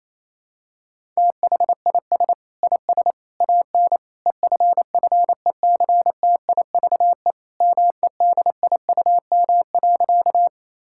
Morse code, heard as a keyed tone, about 28 wpm, 700 Hz. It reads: THIS IS AN EFFECTIVE MEDIUM.
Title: Sentences from Top 1000 Words - Review Part 6 - 28wpm